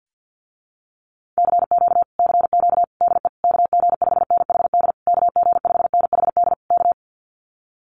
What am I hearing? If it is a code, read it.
Morse code, heard as a keyed tone, about 50 wpm, 700 Hz. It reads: CQ CQ DE KG5NHD KG5NHD K
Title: Common QSO Phrases Encore 50wpm